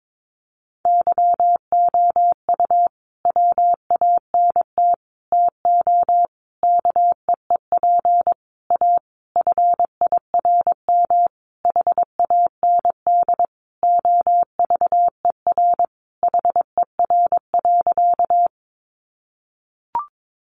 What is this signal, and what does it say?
Morse code, heard as a keyed tone, about 22 wpm, 700 Hz. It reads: YOU WANT TO KEEP A FIRM HAND OVER HER.
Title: Sentences from Top 1400 Words 22wpm